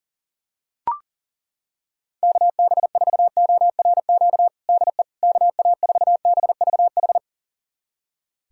Morse code, heard as a keyed tone, about 40 wpm, 700 Hz. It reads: KB4ORQ DE KA4BVH
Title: USA Callsign DE Callsign - 6 Characters - 40wpm